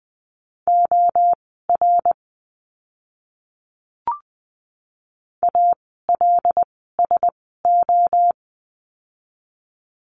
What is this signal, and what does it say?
Morse code, heard as a keyed tone, about 20 wpm, spaced as for 14 wpm, 700 Hz. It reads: OR ALSO
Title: 160 Lesson 26f - Letter g - Words 20wpm